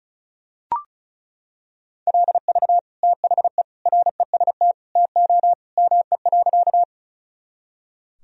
Morse code, heard as a keyed tone, about 34 wpm, 700 Hz. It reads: LV THE REST TO ME.
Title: Sentences from Top 500 Words and Abbreviations 35wpm